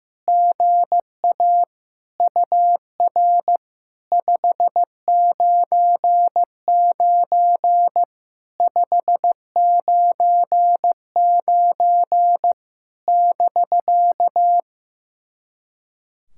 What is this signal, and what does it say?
Morse code, heard as a keyed tone, about 15 wpm, 700 Hz. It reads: GA UR 599 599 <BK>
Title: SOTA QSO Practice 15wpm